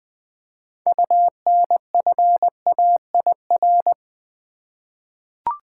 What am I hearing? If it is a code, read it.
Morse code, heard as a keyed tone, about 20 wpm, 700 Hz. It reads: UNFAIR
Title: Six Letter Words - 20wpm